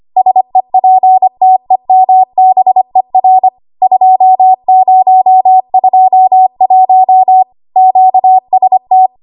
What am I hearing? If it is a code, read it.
Morse code, heard as a keyed tone, about 25 wpm, 750 Hz. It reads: SEPTEMBER 2021 QST
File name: Archive / 25 WPM / 25WPM